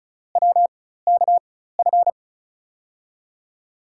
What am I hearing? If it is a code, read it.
Morse code, heard as a keyed tone, about 36 wpm, spaced as for 16 wpm, 700 Hz. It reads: WKF
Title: USA Call Signs - 5 Characters - 4x Character Spacing - 35wpm